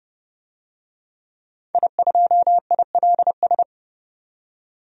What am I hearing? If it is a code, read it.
Morse code, heard as a keyed tone, about 30 wpm, 700 Hz. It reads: I2ILS
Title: DXCC-100 Call Signs - 5 Characters - 30wpm